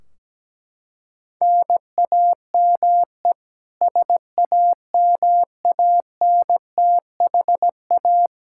Morse code, heard as a keyed tone, about 17 wpm, 700 Hz. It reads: NAME SAMANTHA
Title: Practice QSO - Exchange 1 - Beginning - 17wpm